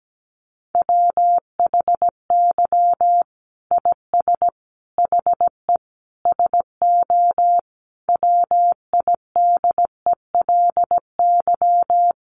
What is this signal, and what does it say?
Morse code, heard as a keyed tone, about 17 wpm, 700 Hz. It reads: WHY IS HE SO WIDELY
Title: Sentences from Top 1200 Words 17wpm